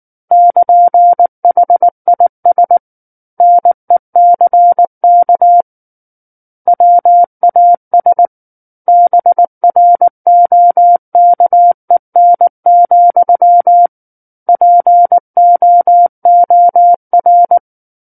Morse code, heard as a keyed tone, about 19 wpm, 700 Hz. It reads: <KN>HIS NECK WAS BROKEN, POOR
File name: Book One / 10-Chapter 10 - 19 WPM